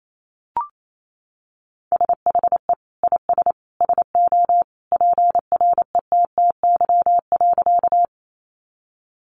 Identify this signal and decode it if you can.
Morse code, heard as a keyed tone, about 28 wpm, 700 Hz. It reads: SHE IS SO PRETTY.
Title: Sentences from Top 1000 Words 28wpm